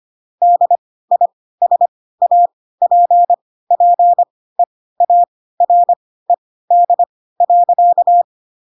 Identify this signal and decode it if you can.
Morse code, heard as a keyed tone, about 25 wpm, spaced as for 16 wpm, 700 Hz. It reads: DISAPPEARED.
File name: Pinocchio / Pinocchio ch7 10 25 ik1hge